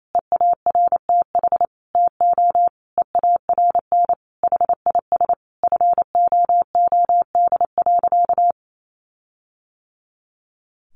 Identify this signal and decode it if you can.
Morse code, heard as a keyed tone, about 28 wpm, 700 Hz. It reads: EARTH TO EARN HIS FOOD.
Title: Sentences from Top 1500 Words 28wpm